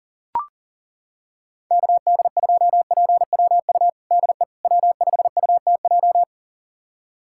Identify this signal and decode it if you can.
Morse code, heard as a keyed tone, about 37 wpm, 700 Hz. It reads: KD2PWU DE W5UTJ
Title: USA Callsign DE Callsign 40wpm